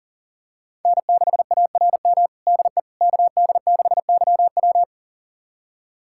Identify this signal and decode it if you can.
Morse code, heard as a keyed tone, about 40 wpm, 700 Hz. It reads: N6ARM DE KD6YW
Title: USA Callsign DE Callsign - 5 Characters - 40wpm